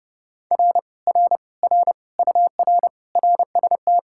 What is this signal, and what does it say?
Morse code, heard as a keyed tone, about 30 wpm, 700 Hz. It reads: R R R UR RST